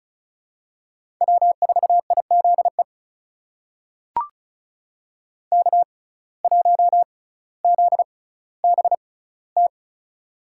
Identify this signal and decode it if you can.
Morse code, heard as a keyed tone, about 36 wpm, spaced as for 18 wpm, 700 Hz. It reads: W4IZE K1ZBT
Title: USA Call Signs - 5 Characters - 6x Character Spacing - 35wpm